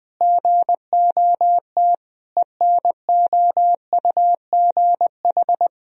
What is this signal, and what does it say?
Morse code, heard as a keyed tone, about 20 wpm, 700 Hz. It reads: GOT ENOUGH
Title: Sentences from Top 1000 Words - Review Part 3 - 20wpm